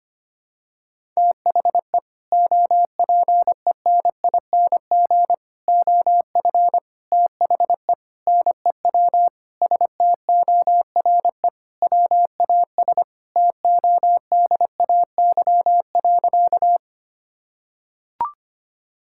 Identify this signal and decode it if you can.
Morse code, heard as a keyed tone, about 25 wpm, 700 Hz. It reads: THE OPENING OF THE NEW STORE WAS TODAY.
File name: practice / 25wpm / Sentences with 3-Word Phrases - 25wpm